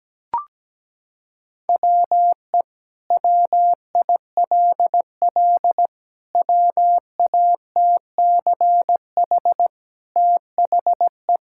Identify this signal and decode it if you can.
Morse code, heard as a keyed tone, about 17 wpm, 700 Hz. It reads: WE WILL WATCH THE
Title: Sentences from Top 1000 Words - Review Part 5 - 17wpm